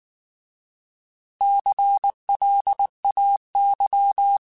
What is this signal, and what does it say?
Morse code, heard as a keyed tone, about 19 wpm, 800 Hz. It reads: CLAY
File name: Alphabetical / Alphabetical mp3s / Cs @18